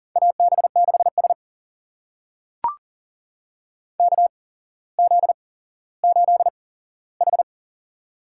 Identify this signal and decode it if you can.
Morse code, heard as a keyed tone, about 39 wpm, spaced as for 17 wpm, 690 Hz. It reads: AB6S KZ8H
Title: USA Call Signs - 4 Characters - 8x Character Spacing - 40wpm